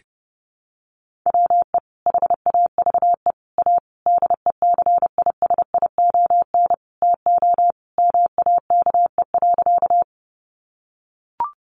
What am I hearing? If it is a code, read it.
Morse code, heard as a keyed tone, about 30 wpm, 700 Hz. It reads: WE HAVE A DECISION TO MAKE.
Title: Sentences from Top 1000 Words 30wpm